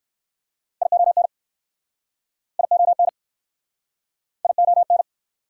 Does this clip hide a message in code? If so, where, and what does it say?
Yes: Morse code from 0.8 to 5.0 s, heard as a keyed tone, about 60 wpm, 690 Hz. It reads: ION ION ION